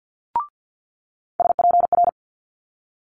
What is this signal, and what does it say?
Morse code, heard as a keyed tone, about 50 wpm, 700 Hz. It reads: SPR
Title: Multi-letter Phonemes Quick Fire 50wpm